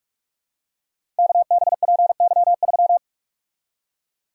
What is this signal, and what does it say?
Morse code, heard as a keyed tone, about 44 wpm, 700 Hz. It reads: KBPY3
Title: Code Groups - 5 Letter:Numbers - 45wpm